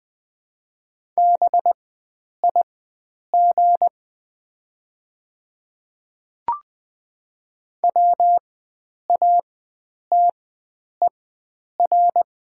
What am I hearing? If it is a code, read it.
Morse code, heard as a keyed tone, about 20 wpm, spaced as for 9 wpm, 700 Hz. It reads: BIG WATER
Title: 4x Character Spacing - Top 300 Words - 20wpm